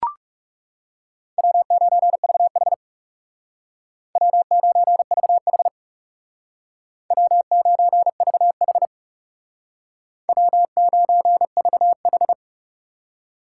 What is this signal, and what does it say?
Morse code, heard as a keyed tone, about 36 wpm, 700 Hz. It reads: W9VH W9VH W9VH W9VH
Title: Speed Racing - USA Call Signs 30wpm